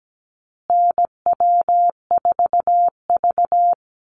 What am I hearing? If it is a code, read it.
Morse code, heard as a keyed tone, about 17 wpm, 700 Hz. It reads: NW4V
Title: Call Signs Repeated 3x - 17wpm